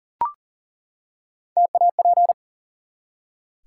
Morse code, heard as a keyed tone, about 40 wpm, 700 Hz. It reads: TAP
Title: Three Letter Words - 40wpm